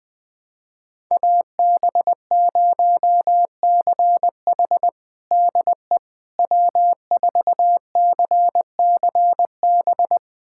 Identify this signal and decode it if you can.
Morse code, heard as a keyed tone, about 20 wpm, 700 Hz. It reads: AB0CH DE W4CCB
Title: USA Callsign DE Callsign - 5 Characters - 20wpm